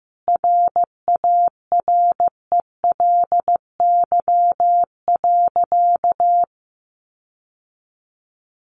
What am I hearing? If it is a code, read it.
Morse code, heard as a keyed tone, about 15 wpm, 700 Hz. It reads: RARELY.
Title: Sentences from Top 1400 Words 15wpm